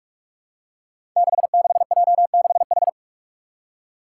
Morse code, heard as a keyed tone, about 44 wpm, 700 Hz. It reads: 66J6H
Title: Five Challenging Letter-Number Combinations 45wpm